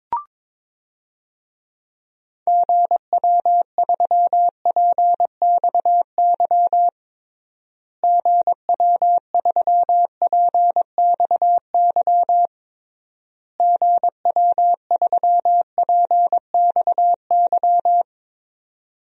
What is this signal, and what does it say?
Morse code, heard as a keyed tone, about 22 wpm, 700 Hz. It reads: GW3PXY GW3PXY GW3PXY
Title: DXCC-100 Call Signs - 6 Characters - Repeated 3x - 22wpm